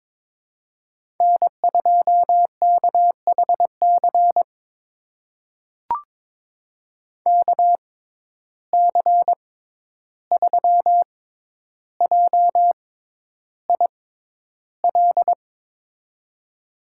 Morse code, heard as a keyed tone, about 22 wpm, spaced as for 11 wpm, 700 Hz. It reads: N2KHC KC3JIL
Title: USA Call Signs - 6x Character Spacing - 22wpm